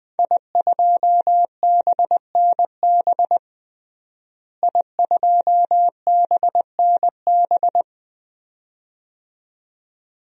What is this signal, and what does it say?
Morse code, heard as a keyed tone, about 20 wpm, 700 Hz. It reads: I2BNB I2BNB
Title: DXCC-100 Call Signs - 5 Characters - Repeated 2x - 20wpm